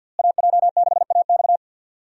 Morse code, heard as a keyed tone, about 46 wpm, 700 Hz. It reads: AJ6AX
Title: USA Call Signs - 5 Characters - Repeated 3x - 50wpm